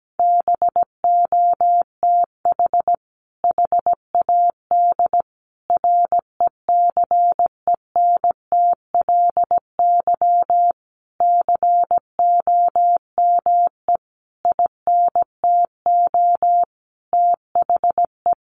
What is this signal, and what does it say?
Morse code, heard as a keyed tone, about 17 wpm, 700 Hz. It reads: BOTH HAD RECENTLY COME INTO THE
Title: Sentences from Top 1200 Words 17wpm